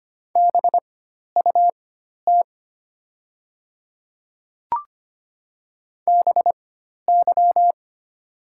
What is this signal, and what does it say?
Morse code, heard as a keyed tone, about 25 wpm, spaced as for 12 wpm, 700 Hz. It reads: BUT BY
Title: 4x Character Spacing - Top 200 Words - 25wpm